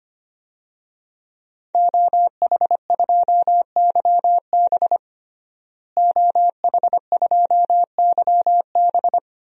Morse code, heard as a keyed tone, about 25 wpm, 700 Hz. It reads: OH2YB OH2YB
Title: DXCC-100 Call Signs - 5 Characters - Repeated 2x - 25wpm